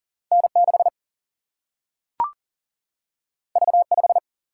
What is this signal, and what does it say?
Morse code, heard as a keyed tone, about 39 wpm, 700 Hz. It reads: N 6 V 5
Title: Single-Letter Single-Number 40wpm